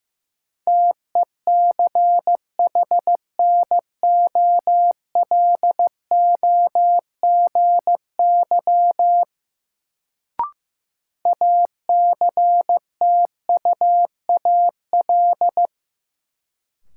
Morse code, heard as a keyed tone, about 15 wpm, 700 Hz. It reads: TECHNOLOGY ACTUAL
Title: Sentences from Top 1000 Words - Review Part 5 - New Word Drill - 15wpm